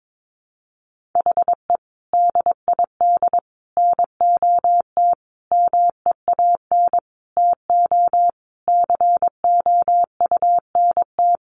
Morse code, heard as a keyed tone, about 22 wpm, 700 Hz. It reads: HE DID NOT MEAN TO COUNT
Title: Sentences from Top 1500 Words 22wpm